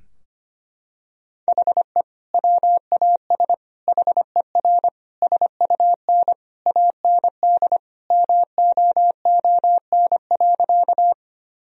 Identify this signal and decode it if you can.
Morse code, heard as a keyed tone, about 25 wpm, 700 Hz. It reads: HE WAS HER SUN AND MOON.
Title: Sentences from Top 1000 Words - Review Part 3 - 25wpm